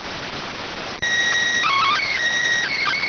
SSTV (slow-scan television), PD 290.